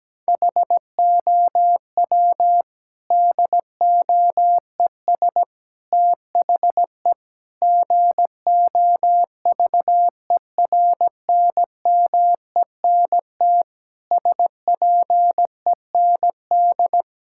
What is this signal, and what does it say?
Morse code, heard as a keyed tone, about 17 wpm, 700 Hz. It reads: HOW DOES THE GOVERNMENT SPEND
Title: Sentences from Top 1000 Words - Review Part 5 - 17wpm